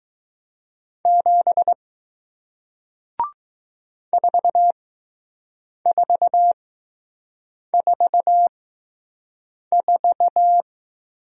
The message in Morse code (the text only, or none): 7 4444